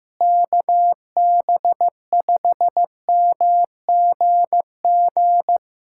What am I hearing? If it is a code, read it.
Morse code, heard as a keyed tone, about 15 wpm, 700 Hz. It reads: KB5MGG
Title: USA Callsign DE Callsign 15wpm